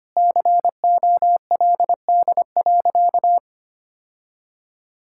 Morse code, heard as a keyed tone, about 25 wpm, 700 Hz. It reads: COLD.
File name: practice / 25wpm / Sentences with 3-Word Phrases - 25wpm